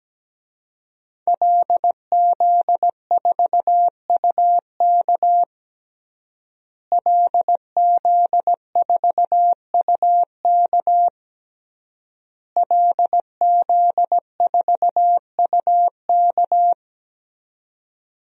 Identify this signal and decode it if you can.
Morse code, heard as a keyed tone, about 17 wpm, 700 Hz. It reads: LZ4UK LZ4UK LZ4UK
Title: DXCC-100 Call Signs - 5 Characters - Repeated 3x - 17wpm